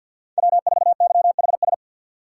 Morse code, heard as a keyed tone, about 51 wpm, 700 Hz. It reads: W4YHS
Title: USA Call Signs - 5 Characters - 3x Character Spacing - 50wpm